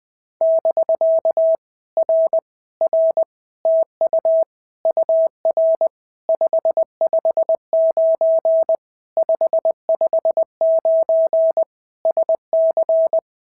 Morse code, heard as a keyed tone, about 20 wpm, 650 Hz. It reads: <BK> R R TU UR 559 559 SC
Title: POTA QSO Practice - 20wpm